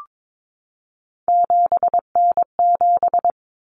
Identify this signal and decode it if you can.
Morse code, heard as a keyed tone, about 22 wpm, 700 Hz. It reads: 7N7
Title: Common Cut Numbers - 3-Digit 22wpm